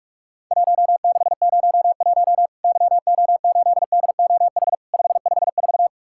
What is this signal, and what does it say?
Morse code, heard as a keyed tone, about 44 wpm, 700 Hz. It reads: 1601 YO8DOH 554
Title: Worked All Europe DX Contest - QTC Exchange - 45wpm